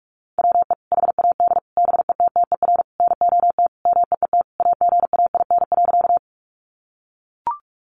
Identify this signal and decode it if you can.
Morse code, heard as a keyed tone, about 44 wpm, 700 Hz. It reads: WE HAD BETTER NOT MEET AGAIN.
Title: Sentences from Top 1100 Words 45wpm